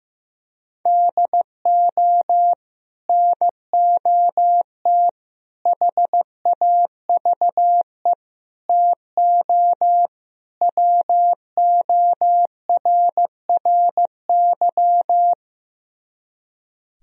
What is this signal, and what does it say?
Morse code, heard as a keyed tone, about 15 wpm, 700 Hz. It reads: DO NOT HAVE TO WORRY
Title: Speed-Racing - Sets of 5 Words 15wpm